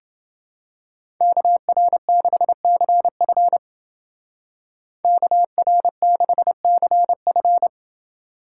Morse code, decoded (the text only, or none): KR6CF KR6CF